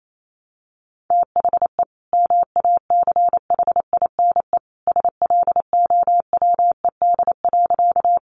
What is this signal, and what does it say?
Morse code, heard as a keyed tone, about 28 wpm, 700 Hz. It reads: THE MACHINE SLOWED.
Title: A Sound Of Thunder 28wpm